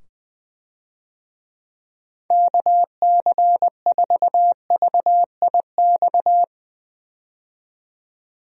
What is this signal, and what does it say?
Morse code, heard as a keyed tone, about 20 wpm, 700 Hz. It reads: KC4VIX
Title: USA Call Signs - 6 Characters - 20wpm